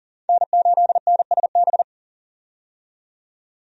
Morse code, heard as a keyed tone, about 39 wpm, 700 Hz. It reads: N8NSB
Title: USA Call Signs - 5 Characters - 40wpm